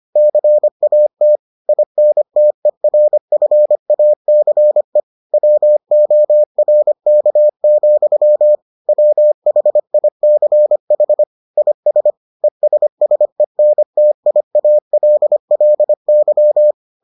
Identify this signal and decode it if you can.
Morse code, heard as a keyed tone, about 25 wpm, 600 Hz. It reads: CAT INTERFACE WORK, WHICH IS ESSENTIALLY